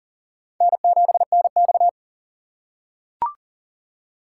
Morse code, heard as a keyed tone, about 40 wpm, 700 Hz. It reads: N7NX
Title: USA Call Signs - 4 Characters - 3x Character Spacing - 40wpm